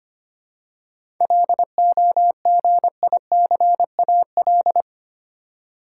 Morse code, heard as a keyed tone, about 25 wpm, 700 Hz. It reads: LOGICAL
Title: Seven Letter Words - 25wpm